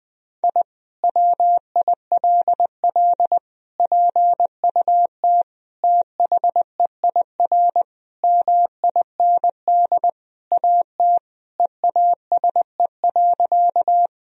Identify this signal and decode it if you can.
Morse code, heard as a keyed tone, about 20 wpm, 700 Hz. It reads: I WILL PUT THEIR MIND AT EASE.
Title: Sentences from Top 1000 Words - Review Part 1 - 20wpm